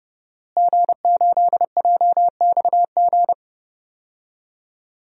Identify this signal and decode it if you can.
Morse code, heard as a keyed tone, about 32 wpm, 700 Hz. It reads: G8JXG
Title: DXCC-100 Call Signs - 5 Characters - Repeated 2x - 30wpm